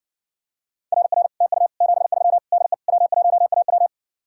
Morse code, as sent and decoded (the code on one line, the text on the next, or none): .-. .-. / - ..- / --... ...-- / -.. / .-- .---- .- .--
RR TU 73 DE W1AW